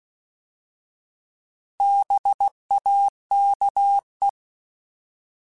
Morse code, heard as a keyed tone, about 16 wpm, 800 Hz. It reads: BAKE
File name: Alphabetical / Alphabetical mp3s / Bs @15